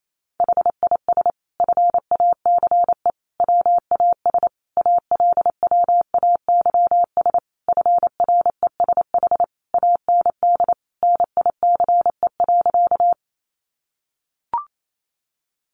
Morse code, heard as a keyed tone, about 28 wpm, 700 Hz. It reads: HIS FACE WAS ALWAYS FRESH AND NICE.
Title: Sentences from Top 1200 Words 28wpm